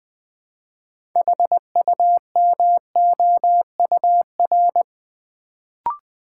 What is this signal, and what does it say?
Morse code, heard as a keyed tone, about 20 wpm, 700 Hz. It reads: HUMOUR
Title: Build A Word - 2 Letters At A Time - 6-Letter Words - 20wpm